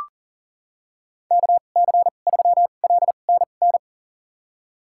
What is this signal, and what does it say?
Morse code, heard as a keyed tone, about 40 wpm, spaced as for 29 wpm, 700 Hz. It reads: KC3LNN
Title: USA Call Signs - 6 Characters - 2x Character Spacing - 40wpm